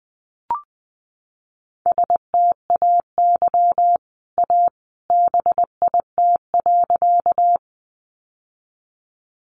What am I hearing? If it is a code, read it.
Morse code, heard as a keyed tone, about 20 wpm, 700 Hz. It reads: STAY A BIT.
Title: Sentences from Top 900 Words 20wpm